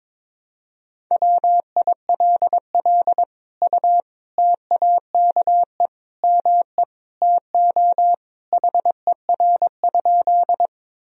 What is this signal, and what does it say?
Morse code, heard as a keyed tone, about 22 wpm, 700 Hz. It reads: WILL U TAKE ME TO HER?
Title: Sentences from Top 200 Words and Abbreviations 22wpm